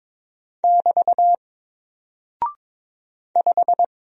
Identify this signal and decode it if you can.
Morse code, heard as a keyed tone, about 22 wpm, 700 Hz. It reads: <BT> 5
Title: Single Character (Letter:Number:Punctuation:Prosign) Alt 22wpm